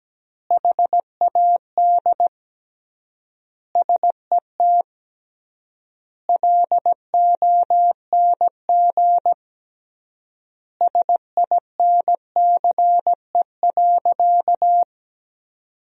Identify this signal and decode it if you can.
Morse code, heard as a keyed tone, about 17 wpm, 700 Hz. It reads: HAD SET LONG SINCE.
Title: 3x Word-Spacing - Sentences from Top 300 Words - 17wpm